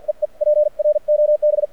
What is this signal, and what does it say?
Morse code, heard as a keyed tone, about 50 wpm, spaced as for 43 wpm, 600 Hz. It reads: EE PROB
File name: archived / CRSnet